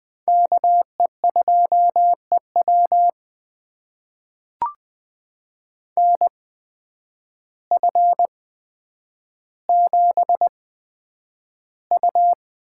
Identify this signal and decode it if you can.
Morse code, heard as a keyed tone, about 20 wpm, spaced as for 9 wpm, 700 Hz. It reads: KE2EW NF7U
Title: USA Call Signs - 8x Character Spacing - 20wpm